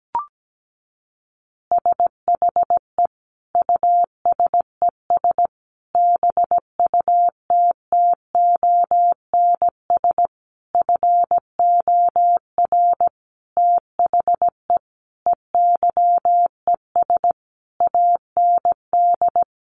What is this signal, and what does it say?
Morse code, heard as a keyed tone, about 17 wpm, 700 Hz. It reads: SHE USES BUTTONS FOR THE EYES AND
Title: Easy Phrases 17wpm